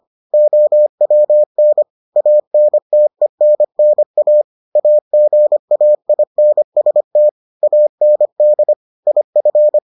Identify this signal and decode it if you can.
Morse code, heard as a keyed tone, about 25 wpm, 600 Hz. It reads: OWN ANTENNA AGAINST AND IF